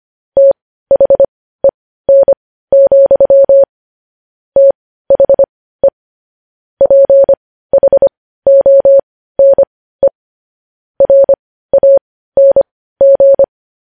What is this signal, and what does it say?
Morse code, heard as a keyed tone, about 25 wpm, spaced as for 15 wpm, 550 Hz. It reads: THEN, THE PHONE RANG
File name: podcast / News headlines-1x-25x15